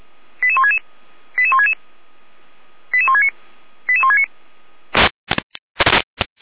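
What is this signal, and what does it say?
Five-tone selective call (ZVEI), heard as quick runs of five tones, tones of about 70 ms. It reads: 88166 88166 88169 88169
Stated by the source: ZVEI five-tone selective call code 88457, received by SDR, from the Feuerwehr Orlamünde fire brigade audio server.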